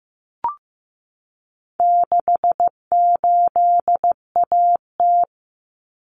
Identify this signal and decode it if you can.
Morse code, heard as a keyed tone, about 15 wpm, 700 Hz. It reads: 68AT
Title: Common Cut Numbers - 4-Digit 15wpm